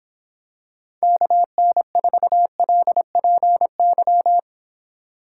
Morse code, heard as a keyed tone, about 25 wpm, 700 Hz. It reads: KN4LPY